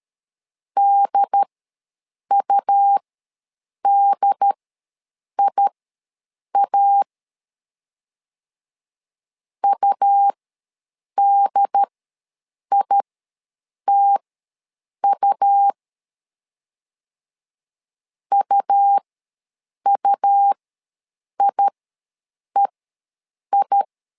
Morse code, heard as a keyed tone, about 13 wpm, spaced as for 7 wpm, 800 Hz. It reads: DUDIA UDITU UUIEI
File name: LEZIONI / 2 UHDN / 2 UHD ws 12wpm 1722